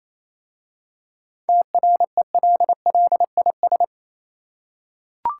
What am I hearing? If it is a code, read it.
Morse code, heard as a keyed tone, about 27 wpm, 700 Hz. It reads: TRELLIS
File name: Easy Words - Speed-Racing with Hints - 22wpm